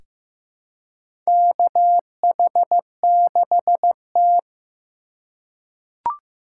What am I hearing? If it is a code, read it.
Morse code, heard as a keyed tone, about 15 wpm, 700 Hz. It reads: KH6T
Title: USA Call Signs - 4 Characters - 3x Character Spacing - 15wpm